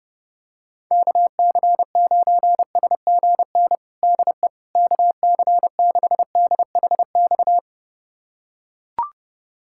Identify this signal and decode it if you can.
Morse code, heard as a keyed tone, about 30 wpm, 700 Hz. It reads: KC9SGN DE KC6DHX
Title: USA Callsign DE Callsign 30wpm